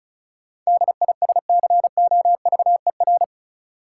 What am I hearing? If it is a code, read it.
Morse code, heard as a keyed tone, about 34 wpm, 700 Hz. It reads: DISCOVER
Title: Sentences from Top 1100 Words - New Word Drill - 35wpm